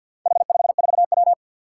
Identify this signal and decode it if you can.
Morse code, heard as a keyed tone, about 52 wpm, 700 Hz. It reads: H54W
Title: Four Challenging Letter-Number Combinations 50wpm